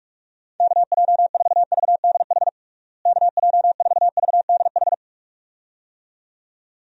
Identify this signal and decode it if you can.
Morse code, heard as a keyed tone, about 46 wpm, 700 Hz. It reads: KJ4VDH KJ4VDH
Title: USA Call Signs - 6 Characters - Repeated 2x - 45wpm